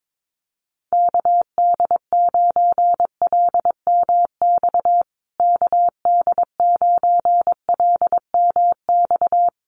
Morse code, heard as a keyed tone, about 22 wpm, 700 Hz. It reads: KD9LMX KD9LMX
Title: USA Call Signs Repeated 3x 22wpm